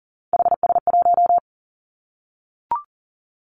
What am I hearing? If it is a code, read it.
Morse code, heard as a keyed tone, about 40 wpm, 700 Hz. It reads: HS1
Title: Three Challenging Letter-Number Combinations 40wpm